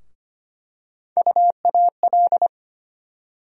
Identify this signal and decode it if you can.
Morse code, heard as a keyed tone, about 25 wpm, 700 Hz. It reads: UAL
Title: USA Call Sign Postfixes - 3 Letters 25wpm